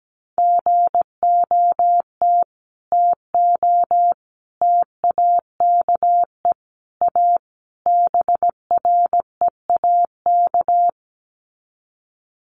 Morse code, heard as a keyed tone, about 17 wpm, 700 Hz. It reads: GOT TO TAKE A BREAK
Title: Sets of 5 Words Encore 17wpm